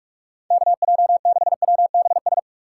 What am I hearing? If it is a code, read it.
Morse code, heard as a keyed tone, about 46 wpm, 690 Hz. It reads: KJ6WBS
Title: USA Call Signs - 6 Characters - 3x Character Spacing - 45wpm